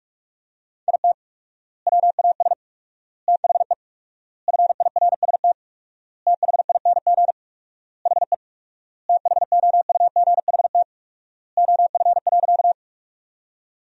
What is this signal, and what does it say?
Morse code, heard as a keyed tone, about 43 wpm, 700 Hz. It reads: IT WAS THE FIRST THING HE THOUGHT OF.
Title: 4x Word-Spacing - Sentences from Top 300 Words - 45wpm